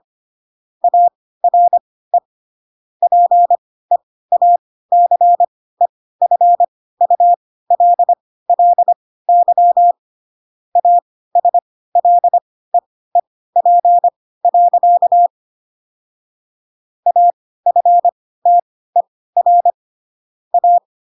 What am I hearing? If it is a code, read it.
Morse code, heard as a keyed tone, about 25 wpm, spaced as for 16 wpm, 700 Hz. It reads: ARE PEACEFULLY ASLEEP. AFTER A